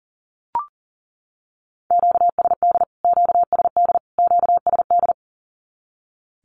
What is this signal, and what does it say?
Morse code, heard as a keyed tone, about 40 wpm, 700 Hz. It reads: QSD QSD QSD
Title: Q-Codes Repeated 3x 40wpm